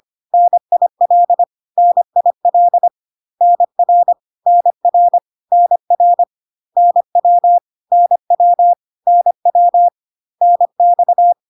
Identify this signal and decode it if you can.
Morse code, heard as a keyed tone, about 25 wpm, 700 Hz. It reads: NIL NIL NR NR NR NW NW NW NX
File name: CW abbreviations alphabetical repeated 3 times 25 25wpm